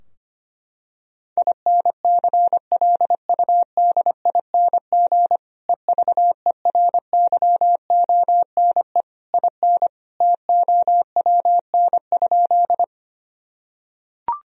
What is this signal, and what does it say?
Morse code, heard as a keyed tone, about 25 wpm, 700 Hz. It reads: INCLUDING EVERYONE IN TOWN?
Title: Sentences from Top 1000 Words - Review Part 4 - 25wpm